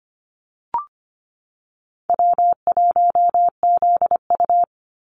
Morse code, heard as a keyed tone, about 25 wpm, 700 Hz. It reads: W1ZU
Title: NA SOTA Activator Callsigns 25wpm